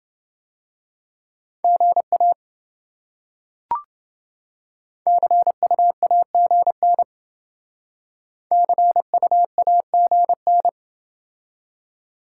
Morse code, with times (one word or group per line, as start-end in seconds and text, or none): 1.6-2.3 GA
5.1-7.0 CUAGN
8.5-10.7 CUAGN